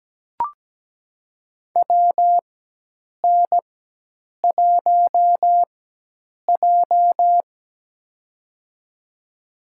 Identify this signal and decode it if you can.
Morse code, heard as a keyed tone, about 17 wpm, spaced as for 8 wpm, 700 Hz. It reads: WN1J
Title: USA Call Signs - 4 Characters - 4x Character Spacing - 17wpm